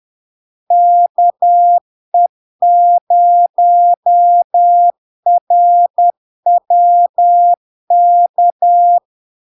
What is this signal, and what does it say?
Morse code, heard as a keyed tone, about 10 wpm, 700 Hz. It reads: KE0RWK
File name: LARK-FD-2022-Log-in-CW-10wpm-700-Hz